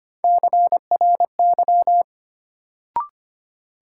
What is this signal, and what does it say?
Morse code, heard as a keyed tone, about 25 wpm, 700 Hz. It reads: CRY
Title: Sets of 5 Words Encore - New Word Drill - 25wpm